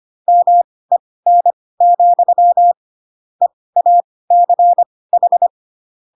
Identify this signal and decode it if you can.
Morse code, heard as a keyed tone, about 25 wpm, spaced as for 18 wpm, 700 Hz. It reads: MEN, EACH
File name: Pinocchio / Pinocchio ch2 12 25 ik1hge